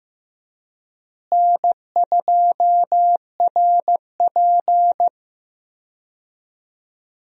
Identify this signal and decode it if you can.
Morse code, heard as a keyed tone, about 16 wpm, 700 Hz. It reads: N2RP
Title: USA Call Signs - 4 Characters - Repeated 3x - 15wpm